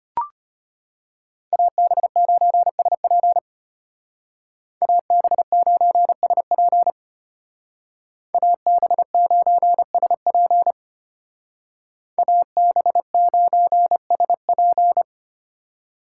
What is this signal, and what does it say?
Morse code, heard as a keyed tone, about 32 wpm, 700 Hz. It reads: AB9SP AB9SP AB9SP AB9SP